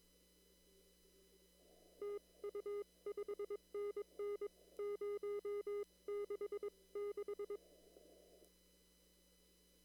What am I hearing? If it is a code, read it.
Morse code, heard as a keyed tone, about 22 wpm, spaced as for 19 wpm, 410 Hz. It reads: TU 5NN 066
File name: CW / kt5j wpx cw-op-k5pi